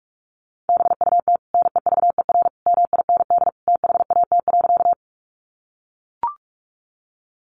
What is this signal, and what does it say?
Morse code, heard as a keyed tone, about 44 wpm, 700 Hz. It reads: BUT NEVER MIND THAT.
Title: Sentences from Top 500 Words 45wpm